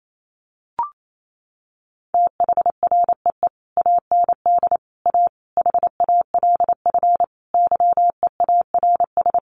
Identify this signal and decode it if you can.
Morse code, heard as a keyed tone, about 28 wpm, 700 Hz. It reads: THREE AND A HALF YEARS
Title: Sets of 5 Words Encore2 28wpm